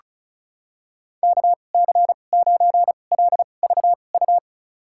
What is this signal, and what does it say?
Morse code, heard as a keyed tone, about 36 wpm, spaced as for 25 wpm, 700 Hz. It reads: KC9LVU
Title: USA Call Signs - 6 Characters - 2x Character Spacing - 35wpm